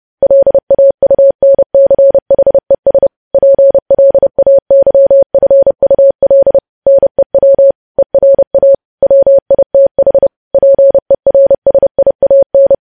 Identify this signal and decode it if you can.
Morse code, heard as a keyed tone, about 30 wpm, 550 Hz. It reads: LAUNCHES PLAYFUL NEW ERA WITH PERSIAN
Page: News Headlines in Morse Code at 30 WPM